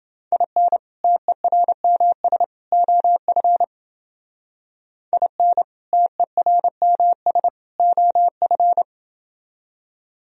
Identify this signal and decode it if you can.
Morse code, heard as a keyed tone, about 30 wpm, 700 Hz. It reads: IN TERMS OF IN TERMS OF